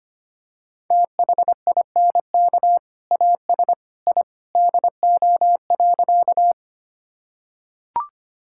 Morse code, heard as a keyed tone, about 25 wpm, 690 Hz. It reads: THINK AS I DO.
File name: practice / 25wpm / Sentences with 2-Word Phrases - 25wpm